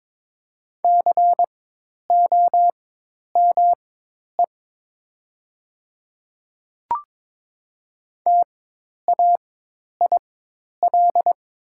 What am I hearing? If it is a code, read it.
Morse code, heard as a keyed tone, about 22 wpm, spaced as for 10 wpm, 700 Hz. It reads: COME TAIL
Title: 4x Character Spacing - Top 500 Words - 22wpm